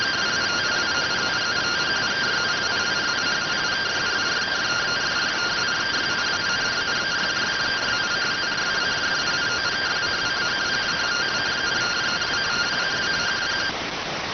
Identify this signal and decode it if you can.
Two-tone teleprinter signal, about 170 Hz shift.
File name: findings / MR AXIAL 33,99mm 1.gen-28 rx